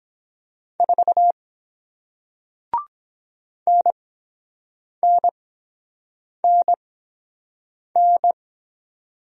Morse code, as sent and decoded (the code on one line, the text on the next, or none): ....- / -. -. -. -.
4 NNNN